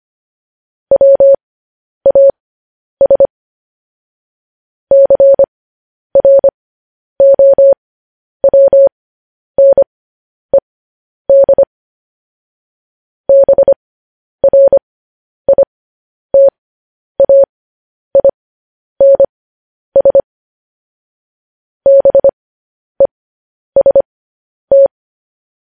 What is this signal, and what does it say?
Morse code, heard as a keyed tone, about 25 wpm, spaced as for 10 wpm, 550 Hz. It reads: WAS CROWNED BRITAINS BEST
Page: news headlines in morse code at 10 wpm